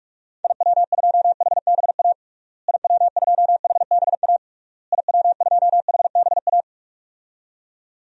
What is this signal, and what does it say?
Morse code, heard as a keyed tone, about 47 wpm, 700 Hz. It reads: IW2HBA IW2HBA IW2HBA
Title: DXCC-100 Call Signs - 6 Characters - Repeated 3x - 45wpm